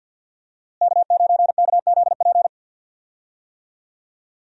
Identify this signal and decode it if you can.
Morse code, heard as a keyed tone, about 51 wpm, 700 Hz. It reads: K9KZP
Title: USA Call Signs - 5 Characters - Repeated 3x - 50wpm